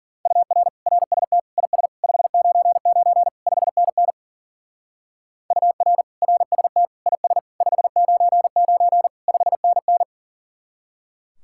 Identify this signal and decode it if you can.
Morse code, heard as a keyed tone, about 48 wpm, 700 Hz. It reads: UR RST IS 599 5NN UR RST IS 599 5NN
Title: Speed-Racing Common QSO Phrases Encore - 40wpm